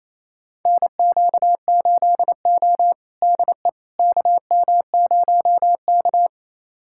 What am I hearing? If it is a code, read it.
Morse code, heard as a keyed tone, about 28 wpm, 700 Hz. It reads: NQ8O DE KM0K
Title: USA Callsign DE Callsign - 4 Characters - 28wpm